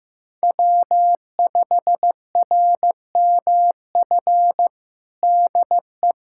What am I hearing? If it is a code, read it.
Morse code, heard as a keyed tone, about 15 wpm, 700 Hz. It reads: W5RMF DE
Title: USA Callsign DE Callsign 15wpm